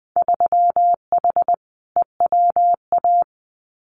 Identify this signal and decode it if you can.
Morse code, heard as a keyed tone, about 20 wpm, 700 Hz. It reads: 3H EWA
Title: Winter Field Day Exchanges Encore 20wpm